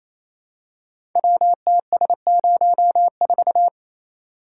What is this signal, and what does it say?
Morse code, heard as a keyed tone, about 29 wpm, 700 Hz. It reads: WTS04
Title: Code Groups - 5 Letter:Numbers - 28wpm